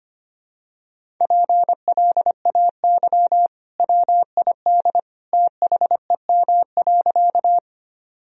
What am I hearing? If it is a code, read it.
Morse code, heard as a keyed tone, about 25 wpm, 700 Hz. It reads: PLAY WID THEM.
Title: Sentences from Top 200 Words and Abbreviations 25wpm